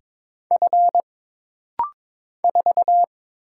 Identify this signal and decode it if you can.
Morse code, heard as a keyed tone, about 22 wpm, 700 Hz. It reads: F 4
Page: This rapid-fire timing is geared to push your ICR (Instant Character Recognition) skills.